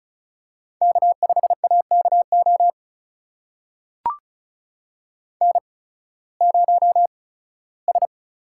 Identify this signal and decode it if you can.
Morse code, heard as a keyed tone, about 36 wpm, spaced as for 19 wpm, 700 Hz. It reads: K5AKO N0S
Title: USA Call Signs - 8x Character Spacing - 35wpm